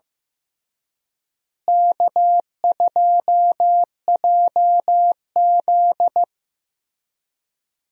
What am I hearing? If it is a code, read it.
Morse code, heard as a keyed tone, about 16 wpm, 700 Hz. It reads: K2JZ
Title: USA Call Signs - 4 Characters - Repeated 3x - 15wpm